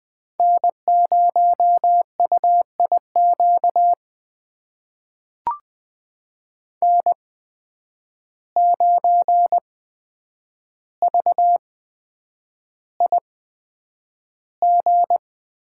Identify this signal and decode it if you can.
Morse code, heard as a keyed tone, about 20 wpm, spaced as for 9 wpm, 700 Hz. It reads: N0UIQ N9VIG
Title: USA Call Signs - 5 Characters - 8x Character Spacing - 20wpm